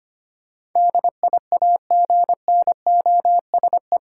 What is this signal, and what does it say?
Morse code, heard as a keyed tone, about 25 wpm, 700 Hz. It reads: DIAGNOSE
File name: 8 Letter Words - First 3 Letters Given - 25wpm